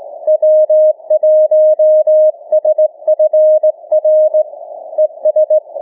Morse code, heard as a keyed tone, about 18 wpm, 620 Hz. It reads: W1SFR ES
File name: cw / ssn 30M